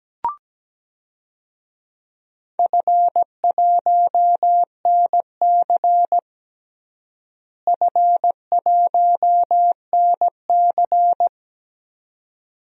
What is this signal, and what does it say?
Morse code, heard as a keyed tone, about 17 wpm, 700 Hz. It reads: F1NC F1NC
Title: DXCC-100 Call Signs - 4 Characters - Repeated 2x - 17wpm